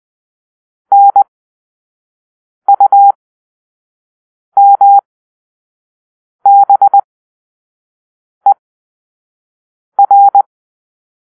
Morse code, heard as a keyed tone, about 20 wpm, spaced as for 5 wpm, 800 Hz. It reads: NUMBER
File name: arrl / Grp-6 Drill-4